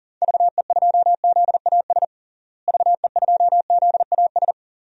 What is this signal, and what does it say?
Morse code, heard as a keyed tone, about 40 wpm, 700 Hz. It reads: VE2ZAS VE2ZAS
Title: DXCC-100 Call Signs - 6 Characters - Repeated 2x - 40wpm